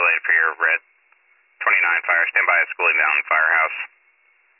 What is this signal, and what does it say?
Emergency Alert System (SAME) tones, heard as harsh buzzing bursts, with speech. It is heard from 0.2 to 3.7 s.